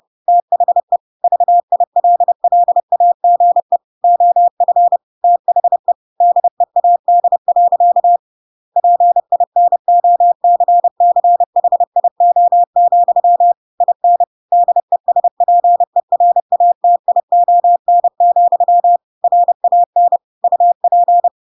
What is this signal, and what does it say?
Morse code, heard as a keyed tone, about 30 wpm, 700 Hz. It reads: THE VILLAGE OF THE DEAD. PINOCCHIO, IN DESPERATION, RAN UP
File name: Pinocchio / Pinocchio ch6 30 30 ik1hge